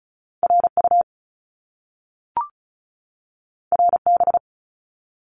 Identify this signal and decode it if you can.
Morse code, heard as a keyed tone, about 36 wpm, 700 Hz. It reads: RU RB
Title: Top Two Letter Combinations in Words 35wpm